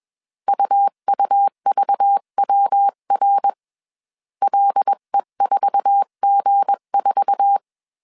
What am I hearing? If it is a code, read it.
Morse code, heard as a keyed tone, about 21 wpm, 800 Hz. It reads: UUVWR LE4G4